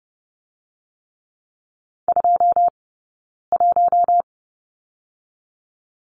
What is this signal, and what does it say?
Morse code, heard as a keyed tone, about 30 wpm, 700 Hz. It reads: 2 1
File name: course / 30:10wpm / 086 Lesson 15d - Number 5 - Two Characters 30wpm